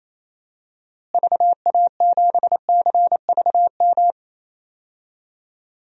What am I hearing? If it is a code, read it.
Morse code, heard as a keyed tone, about 28 wpm, 700 Hz. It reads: VA7CVM
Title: DXCC-100 Call Signs - 6 Characters - Repeated 2x - 28wpm